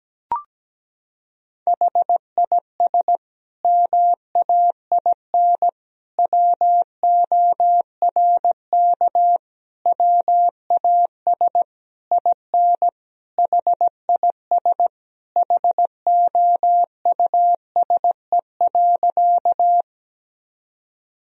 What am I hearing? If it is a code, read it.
Morse code, heard as a keyed tone, about 17 wpm, 700 Hz. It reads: HIS MAIN WORK WAS IN HIS HOUSE.
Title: Sentences from Top 1000 Words - Review Part 2 - 17wpm